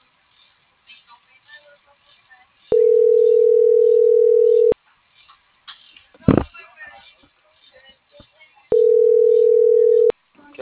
Ringback tone (North America), 440 + 480 Hz.